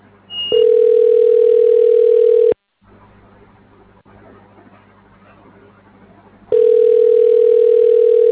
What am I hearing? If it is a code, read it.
Ringback tone (North America), 440 + 480 Hz.